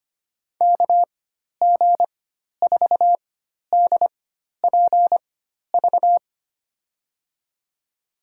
Morse code, heard as a keyed tone, about 25 wpm, spaced as for 12 wpm, 700 Hz. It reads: KG4DPV
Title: USA Call Signs - 6 Characters - 4x Character Spacing - 25wpm